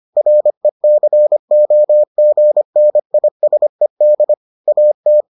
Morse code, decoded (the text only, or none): RECOGNISED AT